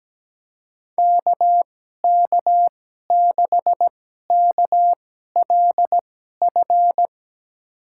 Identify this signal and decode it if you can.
Morse code, heard as a keyed tone, about 17 wpm, spaced as for 12 wpm, 700 Hz. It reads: KK6KLF
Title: USA Call Signs - 6 Characters - 2x Character Spacing - 17wpm